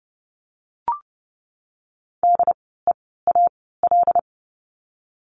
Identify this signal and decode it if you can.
Morse code, heard as a keyed tone, about 30 wpm, spaced as for 17 wpm, 700 Hz. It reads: DEAL
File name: Lesson 16f - Letter c - Words 30wpm